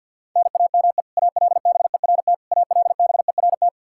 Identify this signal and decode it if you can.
Morse code, heard as a keyed tone, about 51 wpm, 700 Hz. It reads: NAME ALBERT ALBERT
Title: Practice QSO - Exchange 1 - Beginning - 50wpm